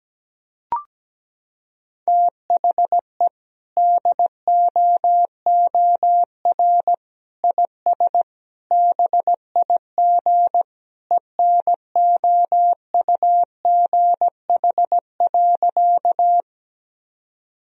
Morse code, heard as a keyed tone, about 17 wpm, 700 Hz. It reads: THE DOOR IS BIG ENOUGH.
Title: Sentences from Top 1000 Words - Review Part 1 - 17wpm